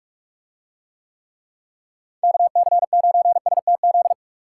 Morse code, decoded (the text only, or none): KC9STZ